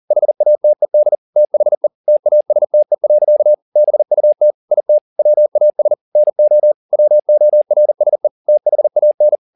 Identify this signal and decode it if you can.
Morse code, heard as a keyed tone, about 40 wpm, 600 Hz. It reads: HATED THE TASTE. BUT IT WAS NO WORSE THAN